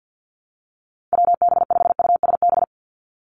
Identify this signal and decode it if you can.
Morse code, heard as a keyed tone, about 50 wpm, 700 Hz. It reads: WB5USB
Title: NA SOTA Activator Callsigns 50wpm